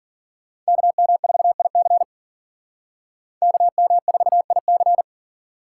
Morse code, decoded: KM4IC KM4IC

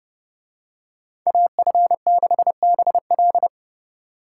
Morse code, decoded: AF6BL

